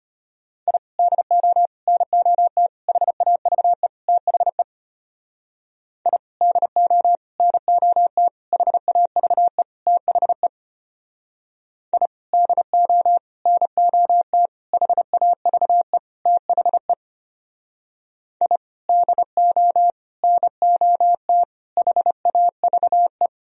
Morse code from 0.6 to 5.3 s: I DO NOT HAVE THE

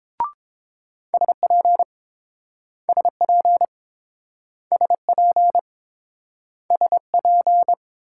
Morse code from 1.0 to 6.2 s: SP SP SP

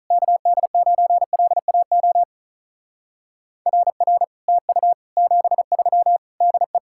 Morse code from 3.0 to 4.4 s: RR

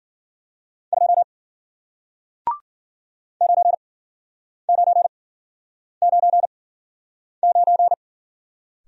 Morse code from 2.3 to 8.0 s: 9999

